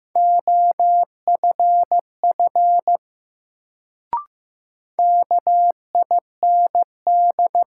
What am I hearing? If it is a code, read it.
Morse code, heard as a keyed tone, about 15 wpm, 700 Hz. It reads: OFF KIND
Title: Sentences from Top 200 Words - New Word Drill - 15wpm